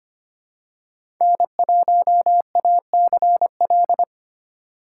Morse code, heard as a keyed tone, about 25 wpm, 700 Hz. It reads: N1ACL
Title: USA Call Signs - 5 Characters - 6x Character Spacing - 25wpm